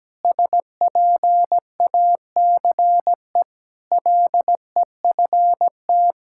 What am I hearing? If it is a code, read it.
Morse code, heard as a keyed tone, about 17 wpm, 700 Hz. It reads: SPACE LEFT
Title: Sentences from Top 1000 Words - Review Part 5 - 17wpm